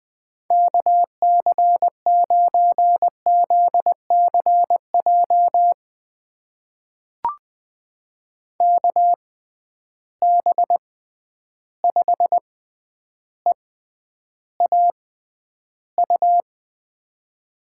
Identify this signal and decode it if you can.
Morse code, heard as a keyed tone, about 20 wpm, spaced as for 10 wpm, 700 Hz. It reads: KC9ZCJ KB5EAU
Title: USA Call Signs - 6 Characters - 6x Character Spacing - 20wpm